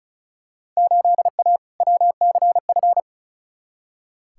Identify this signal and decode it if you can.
Morse code, heard as a keyed tone, about 36 wpm, 690 Hz. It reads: 8A WCF
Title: Field Day Exchanges Encore - 35wpm